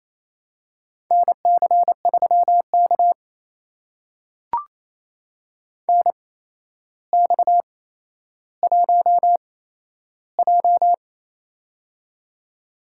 Morse code, heard as a keyed tone, about 28 wpm, spaced as for 12 wpm, 700 Hz. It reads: NC3K NX1J
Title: USA Call Signs - 4 Characters - 8x Character Spacing - 28wpm